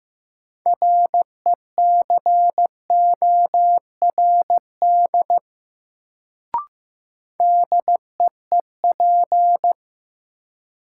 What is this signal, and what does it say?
Morse code, heard as a keyed tone, about 15 wpm, 700 Hz. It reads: RECORD DEEP
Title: Sentences from Top 500 Words - New Word Drill - 15wpm